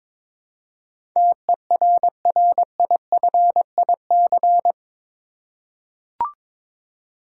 Morse code, heard as a keyed tone, about 22 wpm, 700 Hz. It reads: TERRIFIC
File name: practice / 22wpm / Eight Letter Words - 22wpm